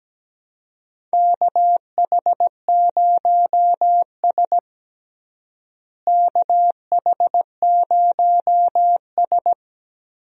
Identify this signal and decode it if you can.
Morse code, heard as a keyed tone, about 17 wpm, 700 Hz. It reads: KH0S KH0S
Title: USA Call Signs - 4 Characters - Repeated 3x - 17wpm